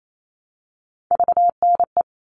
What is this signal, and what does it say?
Morse code, heard as a keyed tone, about 28 wpm, 700 Hz. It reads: VNE
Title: All Cut Numbers - 3-Digit 28wpm